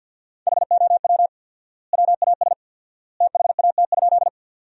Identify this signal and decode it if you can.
Morse code, heard as a keyed tone, about 53 wpm, 700 Hz. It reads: HOW WAS THAT?
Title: 4x Word-Spacing - Sentences from Top 100 Words - 50wpm